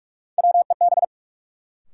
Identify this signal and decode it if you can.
Morse code, heard as a keyed tone, about 46 wpm, 700 Hz. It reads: WEB
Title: Sentences from Top 1000 Words - Review Part 4 - New Word Drill - 45wpm